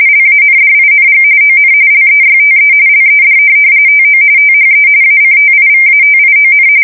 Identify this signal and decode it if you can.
RTTY, about 45.45 baud. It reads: LANDING ON LEYTE LED TO THE ACTIVATION
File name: fsk170 / USS-Massachusetts